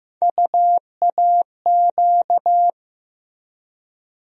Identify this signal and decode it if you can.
Morse code, heard as a keyed tone, about 15 wpm, 700 Hz. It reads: UAQ
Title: USA Call Sign Postfixes - 3 Letters 15wpm